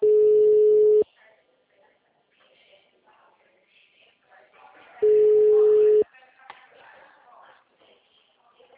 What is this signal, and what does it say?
Ringback tone (Europe, 425 Hz), with speech.
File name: out- -2008- - - .71